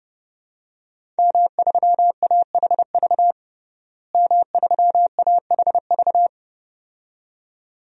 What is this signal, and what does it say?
Morse code, heard as a keyed tone, about 32 wpm, 700 Hz. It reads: M3AHV M3AHV
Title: DXCC-100 Call Signs - 5 Characters - Repeated 2x - 30wpm